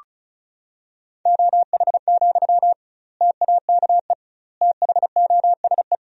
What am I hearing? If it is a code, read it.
Morse code, heard as a keyed tone, about 36 wpm, 700 Hz. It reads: OH, TAKE THOSE
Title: 2x Word-Spacing - Sentences from Top 500 Words - 35wpm